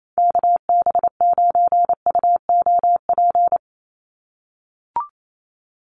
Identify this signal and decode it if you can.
Morse code, heard as a keyed tone, about 28 wpm, 700 Hz. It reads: KB9UOP
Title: USA Call Signs 28wpm